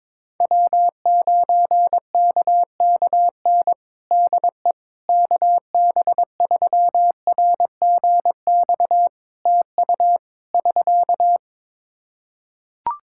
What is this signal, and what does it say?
Morse code, heard as a keyed tone, about 22 wpm, 700 Hz. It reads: W9KKN DE KB3RGX TU <SK>
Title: Practice QSO - Exchange 4 - Ending - 22wpm